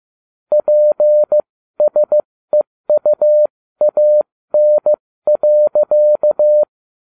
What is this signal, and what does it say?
Morse code, heard as a keyed tone, about 15 wpm, spaced as for 13 wpm, 600 Hz. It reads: PSEUAN.